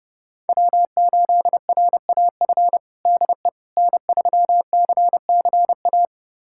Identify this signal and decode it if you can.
Morse code, heard as a keyed tone, about 30 wpm, 700 Hz. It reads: W8RAF DE N3CCA
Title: USA Callsign DE Callsign - 5 Characters - 30wpm